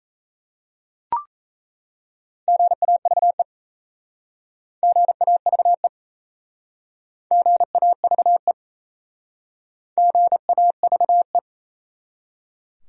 Morse code, heard as a keyed tone, about 37 wpm, 700 Hz. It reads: GAVE GAVE GAVE GAVE